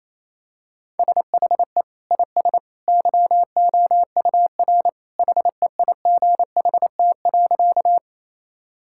Morse code, heard as a keyed tone, about 28 wpm, 700 Hz. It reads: SHE IS YOUR HEIGHT.
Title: Sentences from Top 1600 Words 28wpm